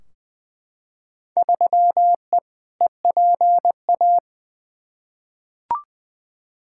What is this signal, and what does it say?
Morse code, heard as a keyed tone, about 20 wpm, 700 Hz. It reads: 3E EPA
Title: Field Day Exchanges Encore - 20wpm